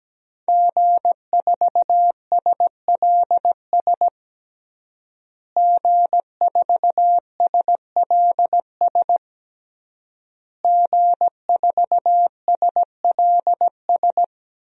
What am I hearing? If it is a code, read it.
Morse code, heard as a keyed tone, about 17 wpm, 700 Hz. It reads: G4SLS G4SLS G4SLS
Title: DXCC-100 Call Signs - 5 Characters - Repeated 3x - 17wpm